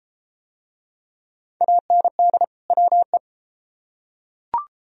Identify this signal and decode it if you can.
Morse code, heard as a keyed tone, about 31 wpm, 700 Hz. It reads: AND WE